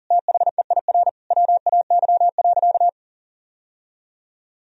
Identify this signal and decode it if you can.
Morse code, heard as a keyed tone, about 37 wpm, 700 Hz. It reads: THEIR WAY.
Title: Sentences from Top 1000 Words - Review Part 1 - 40wpm